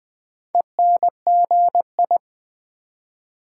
Morse code, heard as a keyed tone, about 20 wpm, 700 Hz. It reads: ENGI
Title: Build A Word - 4 Letters At A Time - 11-Letter Words - 20wpm